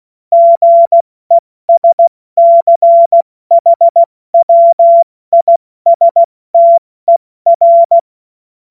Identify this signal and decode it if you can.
Morse code, heard as a keyed tone, about 16 wpm, 680 Hz. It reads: GESCHWISTER
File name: Morsekurs / Witze / Witz 21. März 2023 70 F80